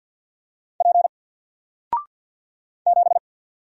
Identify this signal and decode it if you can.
Morse code, heard as a keyed tone, about 50 wpm, 700 Hz. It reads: P 7
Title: Single Letter-Number 50wpm